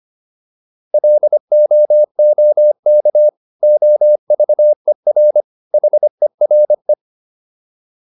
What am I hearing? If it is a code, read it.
Morse code, heard as a keyed tone, about 25 wpm, 600 Hz. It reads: LOOK OVER HERE